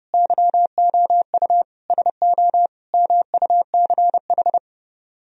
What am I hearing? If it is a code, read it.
Morse code, heard as a keyed tone, about 30 wpm, 700 Hz. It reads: YOU SO MUCH